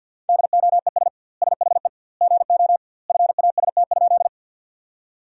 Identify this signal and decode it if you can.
Morse code, heard as a keyed tone, about 50 wpm, 700 Hz. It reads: DOES SHE GO FAST?
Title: 2x Word-Spacing - Sentences from Top 400 Words - 50wpm